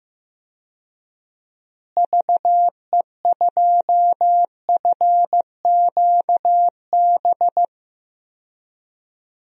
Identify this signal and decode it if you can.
Morse code, heard as a keyed tone, about 15 wpm, 700 Hz. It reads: VE2FQB
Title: DXCC-100 Call Signs - 6 Characters - Repeated 2x - 15wpm